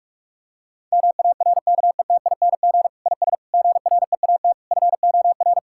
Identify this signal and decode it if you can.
Morse code, heard as a keyed tone, about 46 wpm, 700 Hz. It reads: MARKETING IS GREAT FOR
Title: Sentences from Top 1000 Words - Review Part 5 - 45wpm